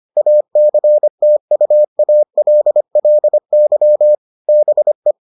Morse code, heard as a keyed tone, about 25 wpm, 600 Hz. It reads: ACTUALLY BE